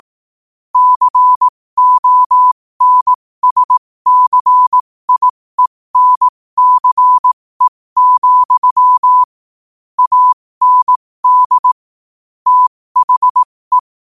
Morse code, heard as a keyed tone, about 18 wpm, spaced as for 15 wpm, 1000 Hz. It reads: CONSCIENCE, AND THE
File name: qotdfiles / 15wpm